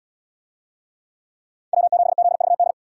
Morse code, heard as a keyed tone, about 77 wpm, 700 Hz. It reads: W7ZVD